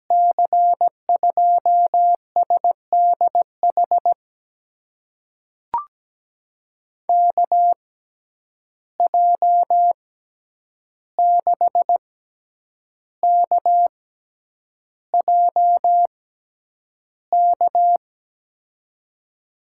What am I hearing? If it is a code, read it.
Morse code, heard as a keyed tone, about 17 wpm, spaced as for 8 wpm, 700 Hz. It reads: C2SDH KJ6KJK
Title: USA Call Signs - 6 Characters - 6x Character Spacing - 17wpm